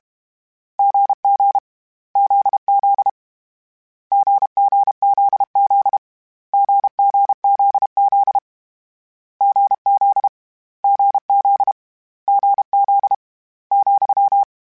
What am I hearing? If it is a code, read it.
Morse code, heard as a keyed tone, about 32 wpm, 800 Hz. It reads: GG ZZ GGZZ GGZZ GZ GZ GZ ,
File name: Difficult Combos / Difficult Combos-mp3s / GZ GZ7 @30